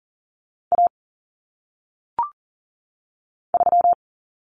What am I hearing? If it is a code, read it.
Morse code, heard as a keyed tone, about 40 wpm, 700 Hz. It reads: A 3
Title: Common Cut Numbers - 1-Digit 40wpm